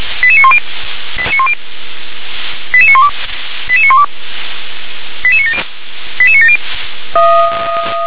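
Five-tone selective call (ZVEI), heard as quick runs of five tones, tones of about 70 ms. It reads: 88910 88912 88912 88970 88970
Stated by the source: ZVEI five-tone selective call received by SDR.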